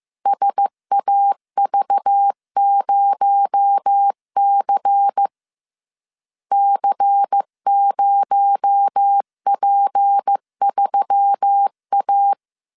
Morse code, heard as a keyed tone, about 15 wpm, 800 Hz. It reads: SAV0C C0P3A
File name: CW / DET / 1 6 1500C S C 14wpm 2951